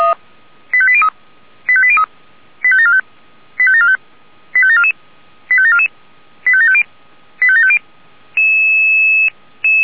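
Five-tone selective call (ZVEI), heard as quick runs of five tones, tones of about 70 ms. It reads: 85803 85803 85745 85745 85744 85744 85750 85750
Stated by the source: ZVEI five-tone selective call received by SDR.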